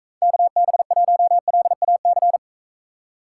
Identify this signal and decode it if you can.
Morse code, heard as a keyed tone, about 41 wpm, 700 Hz. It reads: KB1LAC